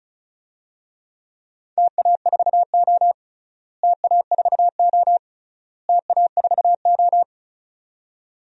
Morse code, heard as a keyed tone, about 36 wpm, 700 Hz. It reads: TA4O TA4O TA4O
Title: DXCC-100 Call Signs - 4 Characters - Repeated 3x - 35wpm